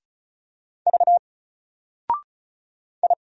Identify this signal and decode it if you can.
Morse code, heard as a keyed tone, about 33 wpm, 700 Hz. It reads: V I
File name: Single-Letter-Number-35wpm